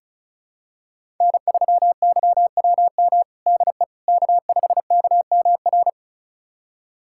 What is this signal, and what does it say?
Morse code, heard as a keyed tone, about 36 wpm, 700 Hz. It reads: N3YWM DE K5KMR
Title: USA Callsign DE Callsign - 5 Characters - 35wpm